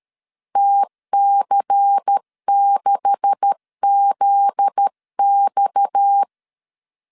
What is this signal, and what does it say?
Morse code, heard as a keyed tone, about 13 wpm, 800 Hz. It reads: TC6ZX
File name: LEZIONI / 9 286 12wpm 2305